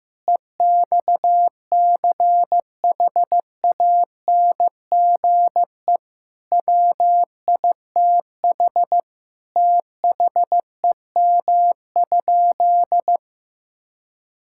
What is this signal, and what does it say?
Morse code, heard as a keyed tone, about 15 wpm, 700 Hz. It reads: EXCHANGE WITH THEM?
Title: Sentences from Top 1000 Words - Review Part 6 - 15wpm